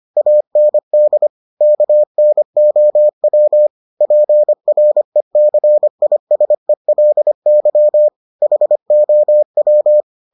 Morse code, heard as a keyed tone, about 25 wpm, 600 Hz. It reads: AND KNOW PRECISELY HOW